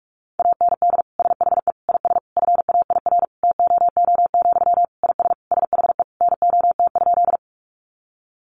Morse code, heard as a keyed tone, about 44 wpm, 700 Hz. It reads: AND SHE IS FAIR TOO, IS SHE NOT?
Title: Sentences from Top 1300 Words 45wpm